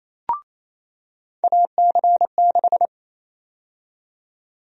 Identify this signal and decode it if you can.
Morse code, heard as a keyed tone, about 28 wpm, 700 Hz. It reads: AC6
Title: USA Call Sign Prefixes - 28wpm